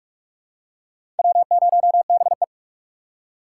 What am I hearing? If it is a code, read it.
Morse code, heard as a keyed tone, about 45 wpm, 700 Hz. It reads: W0BE
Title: USA Call Signs - 4 Characters - 6x Character Spacing - 45wpm